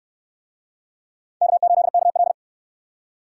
Morse code, heard as a keyed tone, about 64 wpm, 700 Hz. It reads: K8KL